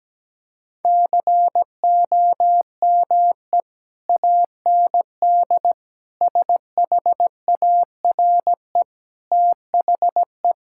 Morse code, heard as a keyed tone, about 17 wpm, 700 Hz. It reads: COME AND SHARE THE
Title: Sentences from Top 1000 Words - Review Part 6 - 17wpm